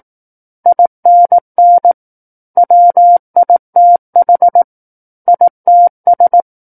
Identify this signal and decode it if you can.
Morse code, heard as a keyed tone, about 18 wpm, 700 Hz. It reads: INN WITH ITS
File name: 18 WPM